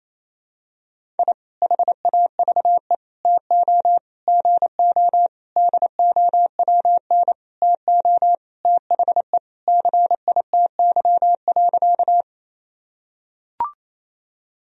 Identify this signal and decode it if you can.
Morse code, heard as a keyed tone, about 28 wpm, 700 Hz. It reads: I HAVE TO GO DOWN TO THE CITY.
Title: Sentences from Top 1000 Words - Review Part 1 - 28wpm